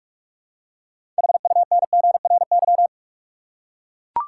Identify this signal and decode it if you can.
Morse code, heard as a keyed tone, about 42 wpm, 700 Hz. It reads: HUNGRY
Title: Six Letter Words - 45wpm